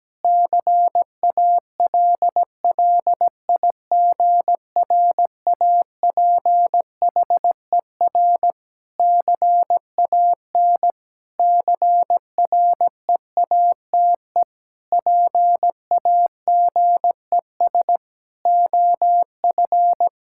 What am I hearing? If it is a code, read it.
Morse code, heard as a keyed tone, about 17 wpm, 700 Hz. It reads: CALLIGRAPHER CAN CREATE PAGES OF
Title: Pangram Challenge v2 - 17wpm